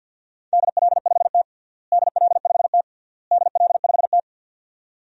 Morse code, heard as a keyed tone, about 51 wpm, 700 Hz. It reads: DL5T DL5T DL5T
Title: DXCC-100 Call Signs - 4 Characters - Repeated 3x - 50wpm